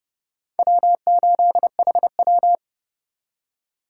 Morse code, heard as a keyed tone, about 31 wpm, 700 Hz. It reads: W8HW
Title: K1USN SST QSO Practice - 30wpm